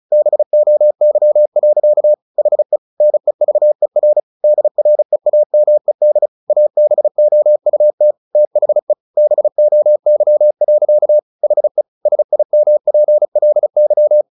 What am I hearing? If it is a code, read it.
Morse code, heard as a keyed tone, about 34 wpm, 600 Hz. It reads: BOY. HE NEVER DREAMED ABOUT THE BOY. HE SIMPLY
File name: the old man and the sea / the old man and the sea 1 35wpm